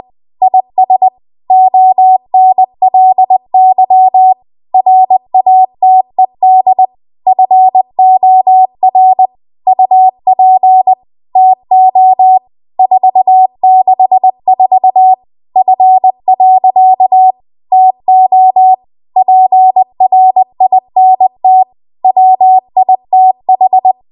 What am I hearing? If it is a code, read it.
Morse code, heard as a keyed tone, about 20 wpm, 750 Hz. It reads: IS ONLY RATED FOR UP TO 464 F. TO PRINT WITH
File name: Archive / 20 WPM / 20WPM